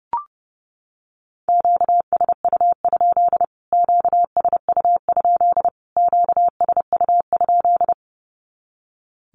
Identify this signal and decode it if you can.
Morse code, heard as a keyed tone, about 30 wpm, 700 Hz. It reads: QSU? QSU? QSU?
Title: Q-Codes Repeated 3x 30wpm